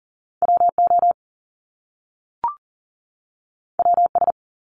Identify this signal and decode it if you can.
Morse code, heard as a keyed tone, about 38 wpm, 700 Hz. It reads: WO WS
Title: Top Two Letter Combinations in Words 40wpm